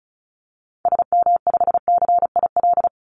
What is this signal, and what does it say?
Morse code, heard as a keyed tone, about 34 wpm, 700 Hz. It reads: SM5CIL
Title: Call Signs - 35wpm